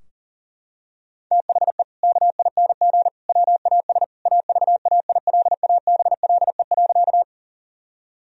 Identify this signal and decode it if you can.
Morse code, heard as a keyed tone, about 40 wpm, 700 Hz. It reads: THE KING WAS AVAILABLE.
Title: Sentences from Top 1000 Words - Review Part 3 - 40wpm